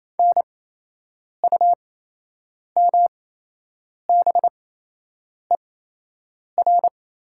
Morse code, heard as a keyed tone, about 28 wpm, spaced as for 8 wpm, 700 Hz. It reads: NUMBER
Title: 8x Character Spacing - Top 200 Words - 28wpm